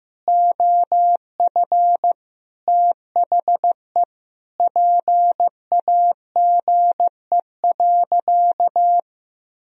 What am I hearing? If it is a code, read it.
Morse code, heard as a keyed tone, about 16 wpm, 700 Hz. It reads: OF THE PAGE.
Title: Sentences from Top 1000 Words - Review Part 2 - 15wpm